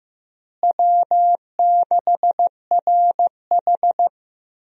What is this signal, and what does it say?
Morse code, heard as a keyed tone, about 16 wpm, 700 Hz. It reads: W6RH
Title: USA Call Signs - 4 Characters - Repeated 3x - 15wpm